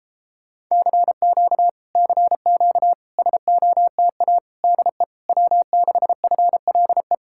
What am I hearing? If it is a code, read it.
Morse code, heard as a keyed tone, about 37 wpm, 700 Hz. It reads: CQ CQ SOTA DE W6FLE